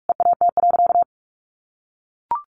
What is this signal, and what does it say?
Morse code, heard as a keyed tone, about 44 wpm, 700 Hz. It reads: EAT.
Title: Sentences from Top 1300 Words 45wpm